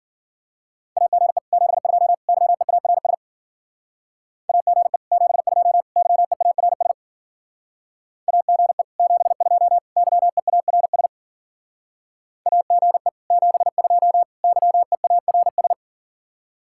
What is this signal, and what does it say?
Morse code, heard as a keyed tone, about 52 wpm, 700 Hz. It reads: AGE 72 YEARS AGE 72 YEARS AGE 72 YEARS AGE 72 YEARS